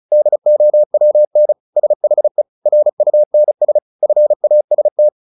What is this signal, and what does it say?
Morse code, heard as a keyed tone, about 34 wpm, 600 Hz. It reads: DOWN SHE RUNS FAST